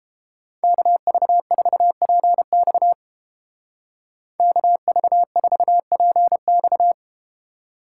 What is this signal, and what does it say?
Morse code, heard as a keyed tone, about 31 wpm, 700 Hz. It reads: KV4PX KV4PX